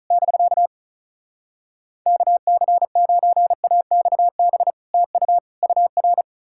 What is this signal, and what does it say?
Morse code, heard as a keyed tone, about 48 wpm, spaced as for 42 wpm, 700 Hz. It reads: <BK> KC9AXB TU UR
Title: Speed-Racing Common QSO Phrases - 35wpm